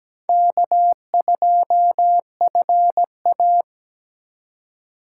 Morse code, heard as a keyed tone, about 17 wpm, 700 Hz. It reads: K2FA
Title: USA Call Signs - 4 Characters - 4x Character Spacing - 17wpm